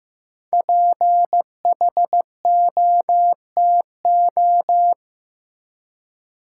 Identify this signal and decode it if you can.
Morse code, heard as a keyed tone, about 16 wpm, 700 Hz. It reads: PHOTO
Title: Five Letter Words - 15wpm